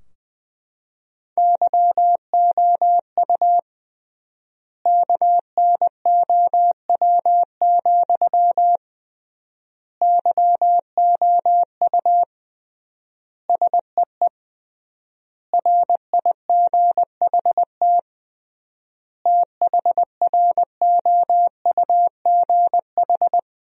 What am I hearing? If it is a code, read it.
Morse code, heard as a keyed tone, about 21 wpm, 700 Hz. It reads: YOU KNOW, YOU SEE RIGHT THROUGH
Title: 3x Word-Spacing - Sentences from Top 200 Words - 20wpm